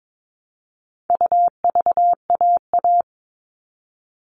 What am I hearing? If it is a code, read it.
Morse code, heard as a keyed tone, about 22 wpm, 700 Hz. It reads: UVAA
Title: All Cut Numbers - 4-Digit 22wpm